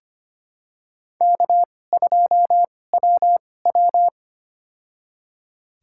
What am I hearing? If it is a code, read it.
Morse code, heard as a keyed tone, about 25 wpm, spaced as for 18 wpm, 700 Hz. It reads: K2WW
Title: USA Call Signs - 4 Characters - 2x Character Spacing - 25wpm